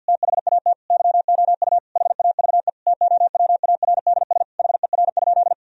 Morse code, heard as a keyed tone, about 50 wpm, 700 Hz. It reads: THAT YOU HAVE TOWARDS HER?
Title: Sentences from Top 1000 Words - Review Part 4 - 50wpm